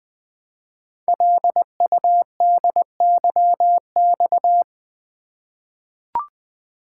Morse code, heard as a keyed tone, about 20 wpm, 700 Hz. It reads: LUDYX
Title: Code Groups - 5 Letters - 20wpm